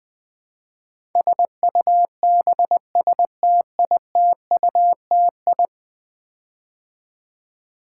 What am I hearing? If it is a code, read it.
Morse code, heard as a keyed tone, about 20 wpm, 700 Hz. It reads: SUBSTITUTI
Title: Build A Word - 5 Letters At A Time - 12-Letter Words - 20wpm